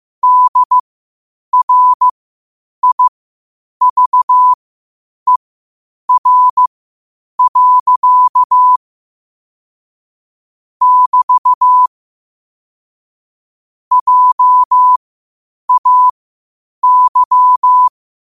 Morse code, heard as a keyed tone, about 15 wpm, spaced as for 8 wpm, 1000 Hz. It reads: DRIVER. <BT> JAY
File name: qotdfiles / 8wpm